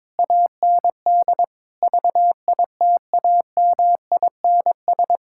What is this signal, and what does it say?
Morse code, heard as a keyed tone, about 22 wpm, 700 Hz. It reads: AND VITAMINS
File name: Difficult Sentences with Hints - 22wpm